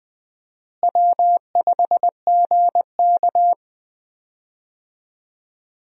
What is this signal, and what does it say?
Morse code, heard as a keyed tone, about 21 wpm, 700 Hz. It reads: W5GK
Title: USA Call Signs - 4 Characters - Repeated 3x - 20wpm